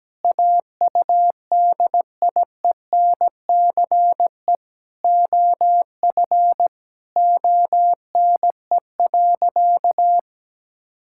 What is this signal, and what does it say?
Morse code, heard as a keyed tone, about 17 wpm, 700 Hz. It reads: AUDIENCE OF ONE.
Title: Sentences from Top 1000 Words - Review Part 5 - 17wpm